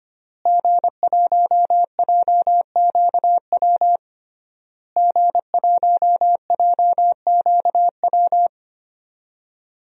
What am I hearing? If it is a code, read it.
Morse code, heard as a keyed tone, about 25 wpm, 700 Hz. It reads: G1JQW G1JQW
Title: DXCC-100 Call Signs - 5 Characters - Repeated 3x - 25wpm